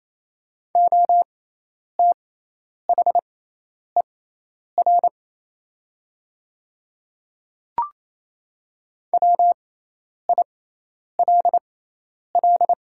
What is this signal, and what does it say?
Morse code, heard as a keyed tone, about 28 wpm, spaced as for 10 wpm, 690 Hz. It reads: OTHER WILL
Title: 6x Character Spacing - Top 300 Words - 28wpm